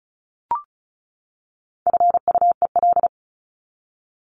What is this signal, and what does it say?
Morse code, heard as a keyed tone, about 34 wpm, 700 Hz. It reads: FUEL
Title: Top 1751-2000 Words 35wpm